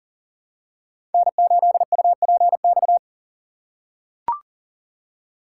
Morse code, heard as a keyed tone, about 40 wpm, 700 Hz. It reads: N8UPX
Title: USA Call Signs - 6x Character Spacing - 40wpm